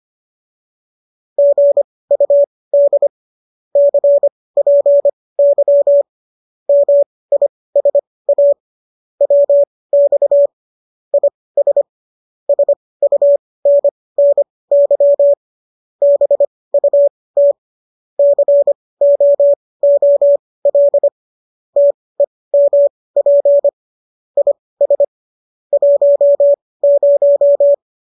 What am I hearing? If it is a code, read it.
Morse code, heard as a keyed tone, about 25 wpm, spaced as for 18 wpm, 570 Hz. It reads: GUD CPY MISA WX IS SUNNY BUT COOL TEMP IS 10